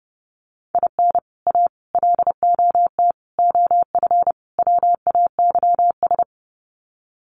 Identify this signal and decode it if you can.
Morse code, heard as a keyed tone, about 30 wpm, 700 Hz. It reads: IN A LOT OF WAYS
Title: Sets of 5 Words Encore 30wpm